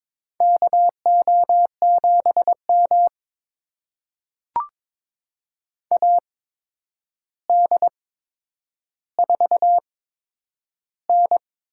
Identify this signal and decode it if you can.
Morse code, heard as a keyed tone, about 21 wpm, spaced as for 9 wpm, 700 Hz. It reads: KO7M AD4N
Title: USA Call Signs - 4 Characters - 8x Character Spacing - 22wpm